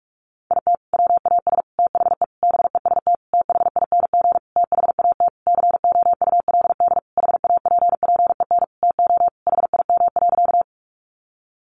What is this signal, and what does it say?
Morse code, heard as a keyed tone, about 44 wpm, 700 Hz. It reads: IT WAS THE BEST THING THAT COULD HAPPEN TO HIM.
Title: Sentences from Top 400 Words 45wpm